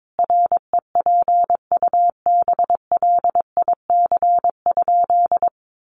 Morse code, heard as a keyed tone, about 22 wpm, 700 Hz. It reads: REPUBLIC?
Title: Sentences from Top 1500 Words 22wpm